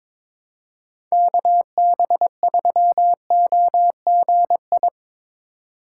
Morse code, heard as a keyed tone, about 22 wpm, 700 Hz. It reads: KB3OGI
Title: USA Call Signs - 4x Character Spacing - 22wpm